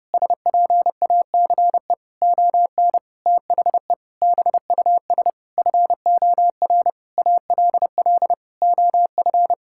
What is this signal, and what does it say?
Morse code, heard as a keyed tone, about 30 wpm, 700 Hz. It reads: SPACE ON THE BUS FOR ALL OF
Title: Sentences from Top 1000 Words - Review Part 6 - 30wpm